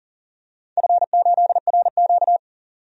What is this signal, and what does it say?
Morse code, heard as a keyed tone, about 40 wpm, 700 Hz. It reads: F8RQ
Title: DXCC-100 Call Signs - 4 Characters - 40wpm